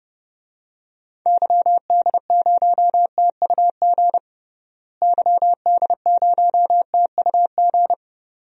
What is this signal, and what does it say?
Morse code, heard as a keyed tone, about 30 wpm, 700 Hz. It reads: YD0TUG YD0TUG
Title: DXCC-100 Call Signs - 6 Characters - Repeated 2x - 30wpm